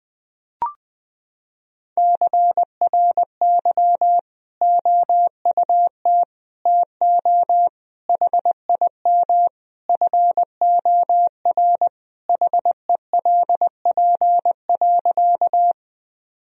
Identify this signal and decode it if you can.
Morse code, heard as a keyed tone, about 20 wpm, 700 Hz. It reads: CRY OUT TO HIM FOR HELP.
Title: Sentences from Top 1000 Words - Review Part 2 - 20wpm